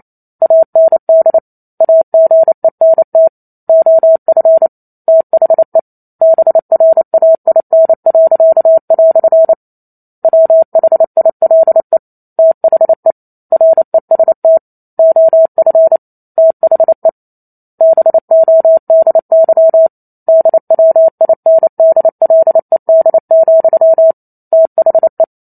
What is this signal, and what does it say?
Morse code, heard as a keyed tone, about 28 wpm, 650 Hz. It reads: AND AGENT OF THE BRAIN." WHILE THE REST OF THE BODY DWINDLED, THE
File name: Chapter 2 - 28 WPM